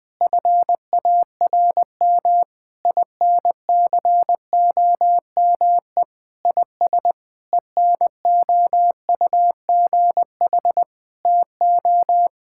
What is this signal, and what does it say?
Morse code, heard as a keyed tone, about 20 wpm, 700 Hz. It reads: FARM INCOME IS ENOUGH TO
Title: Sentences from Top 1000 Words - Review Part 6 - 20wpm